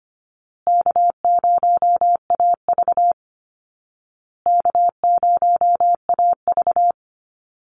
Morse code, heard as a keyed tone, about 25 wpm, 700 Hz. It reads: K0AV K0AV
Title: Easy Call Signs Repeated 3x - 25wpm